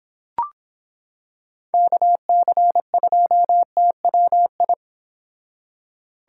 Morse code, heard as a keyed tone, about 26 wpm, 700 Hz. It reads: KC2TWI